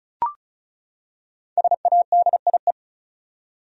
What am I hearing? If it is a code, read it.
Morse code, heard as a keyed tone, about 36 wpm, 700 Hz. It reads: SADIE
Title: USA Common Names - Max 5 Letters - 35wpm